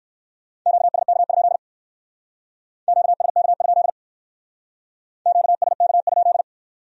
Morse code, heard as a keyed tone, about 54 wpm, 700 Hz. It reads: QSK? QSK? QSK?
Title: Speed-Racing Q-Codes - 45wpm